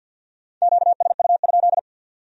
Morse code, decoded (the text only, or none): QSU?